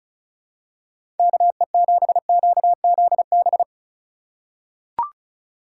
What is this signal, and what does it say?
Morse code, heard as a keyed tone, about 36 wpm, 700 Hz. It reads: KE7QZB
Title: USA Call Signs - 6 Characters - 8x Character Spacing - 35wpm